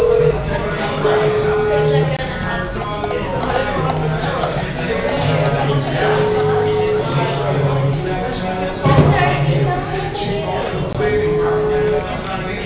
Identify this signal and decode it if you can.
Ringback tone (Europe, 425 Hz), with speech.